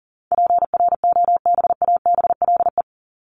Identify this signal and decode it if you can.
Morse code, heard as a keyed tone, about 40 wpm, 700 Hz. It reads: PROBABLE
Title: LY Suffix - Top 100 Words with Base Words 40wpm